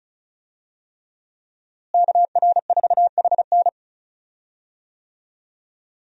Morse code, heard as a keyed tone, about 34 wpm, 700 Hz. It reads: KR4HN
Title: USA Call Signs - 5 Characters - Repeated 3x - 35wpm